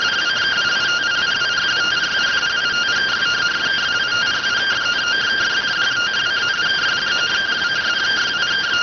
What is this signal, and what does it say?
Two-tone teleprinter signal, about 170 Hz shift.